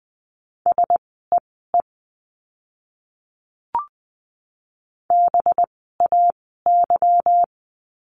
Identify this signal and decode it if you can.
Morse code, heard as a keyed tone, about 20 wpm, spaced as for 15 wpm, 700 Hz. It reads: SEE BAY
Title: 185 Lesson 30f - Letter b - Words 20wpm